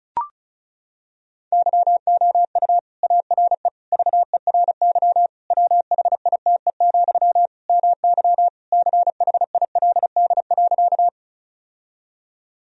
Morse code, heard as a keyed tone, about 35 wpm, 700 Hz. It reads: YOU ARE VERY WHITE, MY CHILD.
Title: Sentences from Top 1000 Words - Review Part 1 - 35wpm